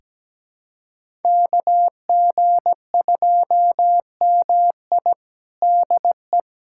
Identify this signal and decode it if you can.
Morse code, heard as a keyed tone, about 17 wpm, 700 Hz. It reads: KG2MI DE
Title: USA Callsign DE Callsign 17wpm